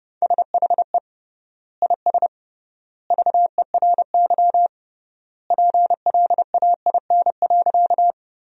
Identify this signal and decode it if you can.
Morse code, heard as a keyed tone, about 30 wpm, 700 Hz. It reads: SHE IS VERY PLAIN.
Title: 3x Word-Spacing - Sentences from Top 400 Words - 30wpm